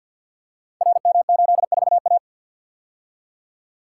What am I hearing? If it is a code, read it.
Morse code, heard as a keyed tone, about 47 wpm, 700 Hz. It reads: RM84A
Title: Code Groups - 5 Letter:Numbers - 50wpm